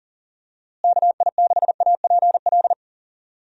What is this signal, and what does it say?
Morse code, heard as a keyed tone, about 40 wpm, 700 Hz. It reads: KI6APL
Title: USA Call Signs - 6 Characters - Repeated 3x - 40wpm